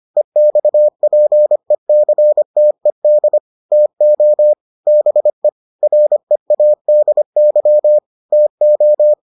Morse code, heard as a keyed tone, about 25 wpm, 600 Hz. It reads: EXPECTED TO BE READY TO